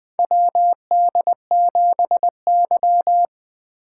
Morse code, heard as a keyed tone, about 20 wpm, 700 Hz. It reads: WD7Y
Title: 2023 SOTA Chaser Call Signs - 20wpm